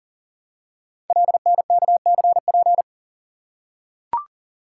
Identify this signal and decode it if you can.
Morse code, heard as a keyed tone, about 41 wpm, 700 Hz. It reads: LNKCP
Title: Code Groups - 5 Letter:Numbers - 40wpm